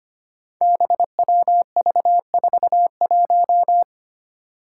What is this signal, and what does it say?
Morse code, heard as a keyed tone, about 25 wpm, 700 Hz. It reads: BWV41
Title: Five Challenging Letter-Number Combinations 25wpm